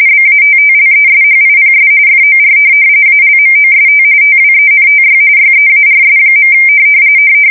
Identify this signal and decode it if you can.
RTTY, about 45.45 baud. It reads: BATTLESHIPS FOR ALMOST TWENTY YEARS.
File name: fsk170 / USS-New-Jersey